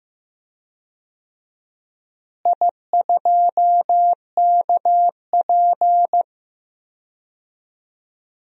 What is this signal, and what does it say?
Morse code, heard as a keyed tone, about 15 wpm, 700 Hz. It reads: I2KP
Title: DXCC-100 Call Signs - 4 Characters - Repeated 3x - 15wpm